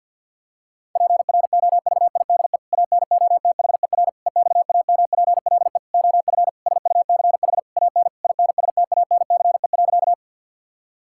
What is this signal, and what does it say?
Morse code, heard as a keyed tone, about 50 wpm, 700 Hz. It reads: PROVIDE ANOTHER EXAMPLE OF SUCH AN INSTANCE.
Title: Sentences from Top 1000 Words - Review Part 5 - 50wpm